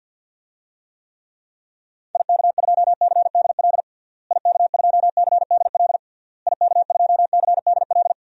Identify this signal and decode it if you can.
Morse code, heard as a keyed tone, about 47 wpm, 700 Hz. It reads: IK2CDL IK2CDL IK2CDL
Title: DXCC-100 Call Signs - 6 Characters - Repeated 3x - 50wpm